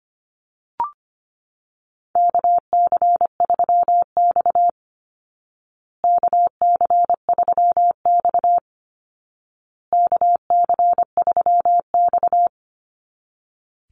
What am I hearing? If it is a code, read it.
Morse code, heard as a keyed tone, about 25 wpm, 700 Hz. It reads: KC3X KC3X KC3X
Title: Call Signs Repeated 2x - 25wpm